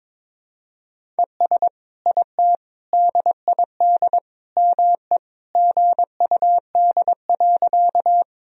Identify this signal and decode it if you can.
Morse code, heard as a keyed tone, about 22 wpm, 700 Hz. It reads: ES IT DID ME GUD.
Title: Sentences from Top 200 Words and Abbreviations 22wpm